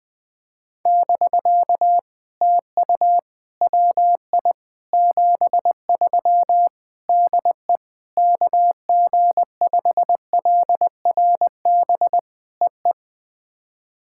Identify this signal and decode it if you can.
Morse code, heard as a keyed tone, about 21 wpm, 700 Hz. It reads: <BK> TU WI 73 DE KG5LRB EE
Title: POTA QSO Practice - 20wpm